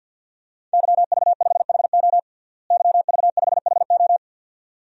Morse code, heard as a keyed tone, about 49 wpm, 700 Hz. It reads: YV5HO YV5HO
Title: DXCC-100 Call Signs - 5 Characters - Repeated 2x - 50wpm